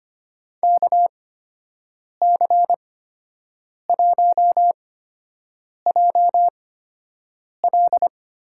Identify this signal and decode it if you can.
Morse code, heard as a keyed tone, about 25 wpm, spaced as for 7 wpm, 700 Hz. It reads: KC1JL
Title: USA Call Signs - 6 Characters - 8x Character Spacing - 25wpm